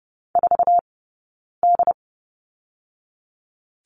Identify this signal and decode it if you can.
Morse code, heard as a keyed tone, about 30 wpm, 700 Hz. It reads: 4 D
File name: Lesson 16d - Letter c - Two Characters 30wpm